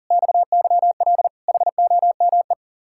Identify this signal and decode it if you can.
Morse code, heard as a keyed tone, about 48 wpm, 700 Hz. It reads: XYL HOME
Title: Speed-Racing Common QSO Phrases - 40wpm